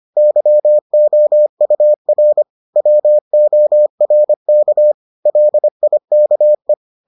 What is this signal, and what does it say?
Morse code, heard as a keyed tone, about 25 wpm, 600 Hz. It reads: YOUR WORK LIKE